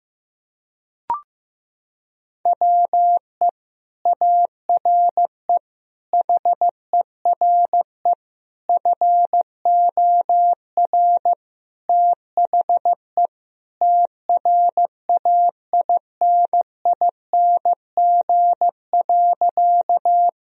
Morse code, heard as a keyed tone, about 15 wpm, 700 Hz. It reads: WE ARE HERE FOR THE TRAINING.
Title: Sentences from Top 1000 Words - Review Part 4 - 15wpm